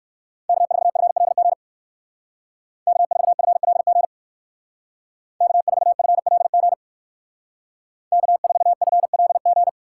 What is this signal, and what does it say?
Morse code, heard as a keyed tone, about 57 wpm, 700 Hz. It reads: K4FLG K4FLG K4FLG K4FLG